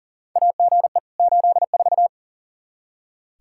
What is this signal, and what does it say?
Morse code, heard as a keyed tone, about 40 wpm, 690 Hz. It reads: AGE 84
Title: Practice QSO - Exchange 3 - More Info - 40wpm